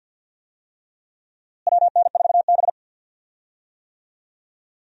Morse code, heard as a keyed tone, about 50 wpm, 700 Hz. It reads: WN4B
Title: USA Call Signs - 4 Characters - Repeated 3x - 50wpm